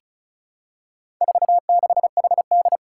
Morse code, heard as a keyed tone, about 36 wpm, 690 Hz. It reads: 46HD
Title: Four Challenging Letter-Number Combinations 35wpm